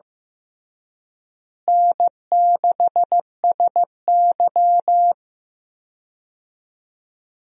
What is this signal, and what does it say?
Morse code, heard as a keyed tone, about 15 wpm, 700 Hz. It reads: N6SY
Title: USA Call Signs - 4 Characters - Repeated 3x - 15wpm